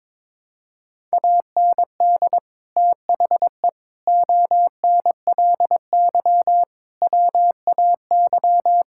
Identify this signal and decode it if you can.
Morse code, heard as a keyed tone, about 22 wpm, 700 Hz. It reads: AND THE ONLY WAY
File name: Sentences with 9-Word Phrases - 22wpm